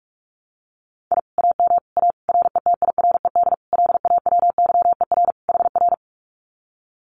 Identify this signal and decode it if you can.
Morse code, heard as a keyed tone, about 46 wpm, 700 Hz. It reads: I AM A RETIRED LAWYER HR
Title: Common QSO Phrases Encore 45wpm